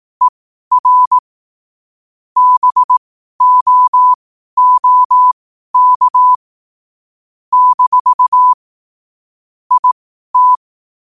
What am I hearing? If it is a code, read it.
Morse code, heard as a keyed tone, about 18 wpm, spaced as for 12 wpm, 1000 Hz. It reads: ER BOOK - IT
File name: qotdfiles / 12wpm